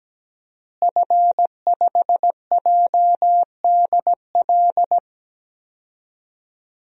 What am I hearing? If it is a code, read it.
Morse code, heard as a keyed tone, about 17 wpm, 700 Hz. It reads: F5JDL
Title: DXCC-100 Call Signs - 5 Characters - Repeated 2x - 17wpm